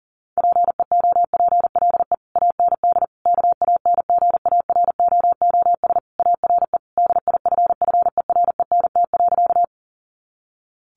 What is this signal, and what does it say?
Morse code, heard as a keyed tone, about 40 wpm, 700 Hz. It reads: PEOPLE AND KANGAROOS ARE DIFFERENT.
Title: Easy Phrases 40wpm